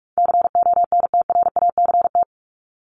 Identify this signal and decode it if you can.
Morse code, heard as a keyed tone, about 44 wpm, 700 Hz. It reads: CONTRACT
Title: Top 1251-1500 Words 45wpm